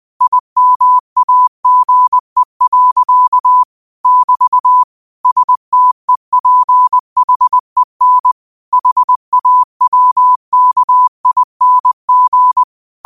Morse code, heard as a keyed tone, about 20 wpm, 1000 Hz. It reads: IMAGE. <BT> STEPHEN HAWKING
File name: qotdfiles / 20wpm